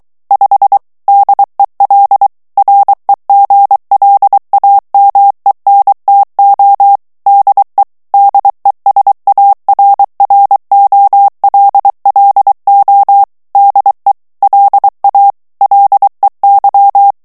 Morse code, heard as a keyed tone, about 23 wpm, 800 Hz. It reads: 5 DEL REGLAMENTO DE DESARROLLO DE LA LEY
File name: morse mp3 / ESPAGNOL TEXTE EN CLAIR 2 VITESSE 22 MOTS MINUTES